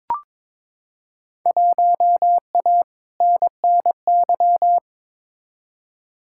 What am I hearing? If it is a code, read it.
Morse code, heard as a keyed tone, about 22 wpm, 700 Hz. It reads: 1A NNY
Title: Field Day Exchanges - 22wpm